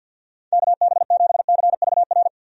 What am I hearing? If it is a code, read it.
Morse code, heard as a keyed tone, about 51 wpm, 700 Hz. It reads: KB7CVR
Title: USA Call Signs - 6 Characters - 50wpm